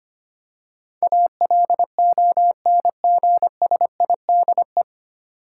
Morse code, heard as a keyed tone, about 25 wpm, 690 Hz. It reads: ALONGSIDE
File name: practice / 25wpm / 9 Letter Words - First 3 Letters Given - 25wpm